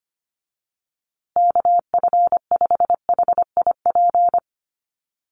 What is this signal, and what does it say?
Morse code, heard as a keyed tone, about 25 wpm, 700 Hz. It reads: KF5HIP
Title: USA Call Signs Repeated 3x 25wpm